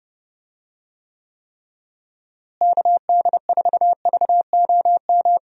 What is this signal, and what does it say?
Morse code, heard as a keyed tone, about 32 wpm, 700 Hz. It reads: KD4VOM
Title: USA Call Signs - 6 Characters - Repeated 3x - 30wpm